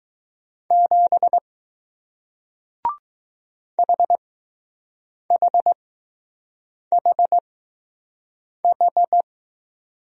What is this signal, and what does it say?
Morse code, heard as a keyed tone, about 20 wpm, spaced as for 6 wpm, 700 Hz. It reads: 7 HHHH